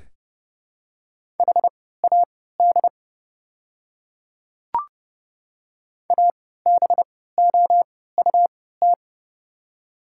Morse code, heard as a keyed tone, about 30 wpm, spaced as for 17 wpm, 690 Hz. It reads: HAD ABOUT
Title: 3x Character Spacing - Top 100 Words - 30wpm